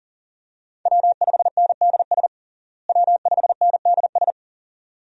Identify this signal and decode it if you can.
Morse code, heard as a keyed tone, about 40 wpm, 700 Hz. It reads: W5NDS W5NDS
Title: USA Call Signs - 5 Characters - Repeated 2x - 40wpm